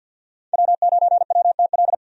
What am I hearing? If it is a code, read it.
Morse code, heard as a keyed tone, about 46 wpm, 700 Hz. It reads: W9WTL
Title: USA Call Signs - 5 Characters - Repeated 2x - 50wpm